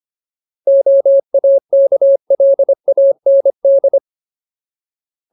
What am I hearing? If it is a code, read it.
Morse code, heard as a keyed tone, about 25 wpm, 550 Hz. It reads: OAKLAND
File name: Ultimate Word List / 25WPM Ultimate